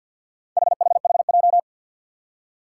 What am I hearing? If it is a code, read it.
Morse code, heard as a keyed tone, about 52 wpm, 700 Hz. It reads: HHHJ
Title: Four Challenging Letter-Number Combinations 50wpm